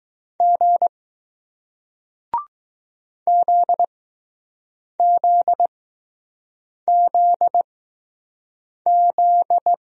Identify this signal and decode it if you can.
Morse code, heard as a keyed tone, about 20 wpm, spaced as for 6 wpm, 700 Hz. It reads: G ZZZZ